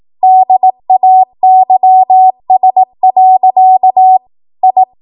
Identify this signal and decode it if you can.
Morse code, heard as a keyed tone, about 18 wpm, 750 Hz. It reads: DAYS. I
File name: Archive / 18 WPM / 18WPM